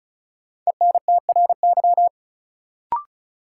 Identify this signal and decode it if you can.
Morse code, heard as a keyed tone, about 34 wpm, 700 Hz. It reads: ENTRY
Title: Sentences from Top 1500 Words - New Word Drill - 35wpm